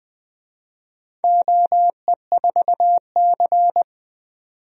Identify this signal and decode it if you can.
Morse code, heard as a keyed tone, about 20 wpm, 700 Hz. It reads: OE4C
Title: DXCC-100 Call Signs - 4 Characters - 20wpm